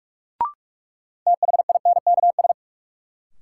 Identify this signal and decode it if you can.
Morse code, heard as a keyed tone, about 46 wpm, 700 Hz. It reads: THINKS
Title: Sentences from Top 1000 Words - Review Part 6 - New Word Drill - 45wpm